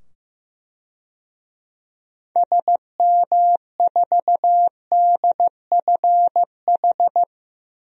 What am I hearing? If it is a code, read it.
Morse code, heard as a keyed tone, about 15 wpm, 700 Hz. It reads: SM4DFH
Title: DXCC-100 Call Signs - 6 Characters - 15wpm